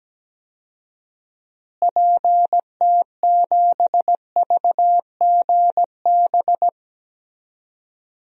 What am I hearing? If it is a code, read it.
Morse code, heard as a keyed tone, about 17 wpm, 700 Hz. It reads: PT7VGB
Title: DXCC-100 Call Signs - 6 Characters - Repeated 2x - 17wpm